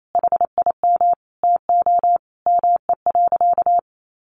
Morse code, heard as a keyed tone, about 28 wpm, 700 Hz. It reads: HIM TO ME.
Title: Sentences from Top 1500 Words 28wpm